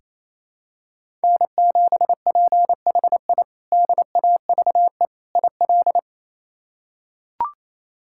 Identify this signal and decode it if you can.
Morse code, heard as a keyed tone, about 28 wpm, 700 Hz. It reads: N7PHI DAVE IL
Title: K1USN SST QSO Practice - 28wpm